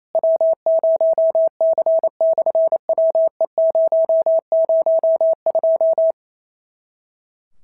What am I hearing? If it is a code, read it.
Morse code, heard as a keyed tone, about 28 wpm, 650 Hz. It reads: W0C/WE002
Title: SOTA QSO Practice 28wpm